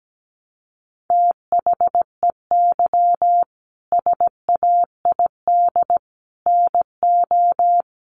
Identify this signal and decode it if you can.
Morse code, heard as a keyed tone, about 17 wpm, 700 Hz. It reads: THEY SAID NO
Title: Sentences from Top 100 Words 17wpm